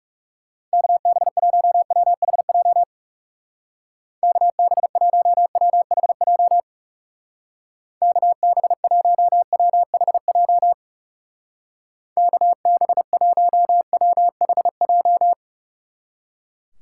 Morse code, heard as a keyed tone, about 39 wpm, 700 Hz. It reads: KB1WHJ KB1WHJ KB1WHJ KB1WHJ